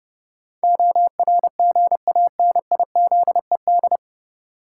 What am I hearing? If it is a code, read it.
Morse code, heard as a keyed tone, about 31 wpm, 700 Hz. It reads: ORGANIZED
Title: Sentences from Top 1300 Words - New Word Drill Alt - 30wpm